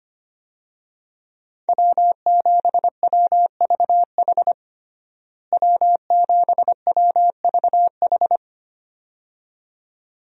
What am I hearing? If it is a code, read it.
Morse code, heard as a keyed tone, about 25 wpm, 700 Hz. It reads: W7WVH W7WVH
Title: USA Call Signs - 5 Characters - Repeated 2x - 25wpm